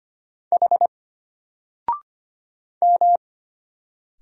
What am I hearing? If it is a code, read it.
Morse code, heard as a keyed tone, about 25 wpm, 700 Hz. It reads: H M
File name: Single-Letter-Number-25wpm